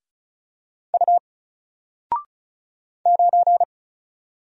Morse code, heard as a keyed tone, about 36 wpm, 700 Hz. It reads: U 9
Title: Single Character (Letter:Number:Punctuation:Prosign) Alt 35wpm